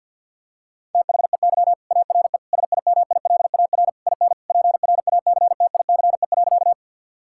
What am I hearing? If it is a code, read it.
Morse code, heard as a keyed tone, about 52 wpm, 700 Hz. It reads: THEY ARE SIMILAR IN PRACTICE.
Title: Sentences from Top 1000 Words - Review Part 5 - 50wpm